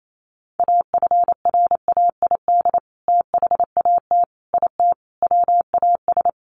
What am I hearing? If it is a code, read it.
Morse code, heard as a keyed tone, about 28 wpm, 700 Hz. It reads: AFRAID THAT IT WAS
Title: Sentences from Top 1500 Words 28wpm